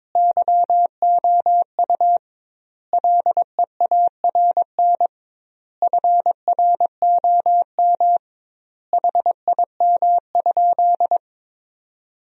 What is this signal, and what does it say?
Morse code, heard as a keyed tone, about 21 wpm, 700 Hz. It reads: YOU LEARN FROM HIM?
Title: 2x Word-Spacing - Sentences from Top 300 Words - 22wpm